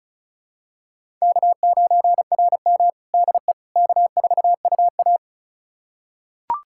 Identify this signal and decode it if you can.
Morse code, heard as a keyed tone, about 36 wpm, 700 Hz. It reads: K9RM DE K4UA
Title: USA Callsign DE Callsign - 4 Characters - 35wpm